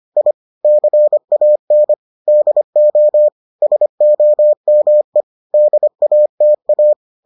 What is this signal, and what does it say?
Morse code, heard as a keyed tone, about 25 wpm, 600 Hz. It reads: I CAN DO SOME DATA